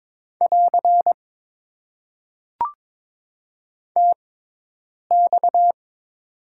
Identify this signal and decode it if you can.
Morse code, heard as a keyed tone, about 22 wpm, spaced as for 8 wpm, 700 Hz. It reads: <AR> TX
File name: Top 90 QSO Elements - 6x Character Spacing - 22wpm